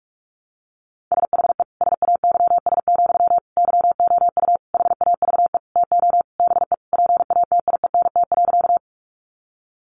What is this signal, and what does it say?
Morse code, heard as a keyed tone, about 44 wpm, 700 Hz. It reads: SHE SAYS, YOU HAVE TO BE PATIENT.
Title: Easy Phrases 45wpm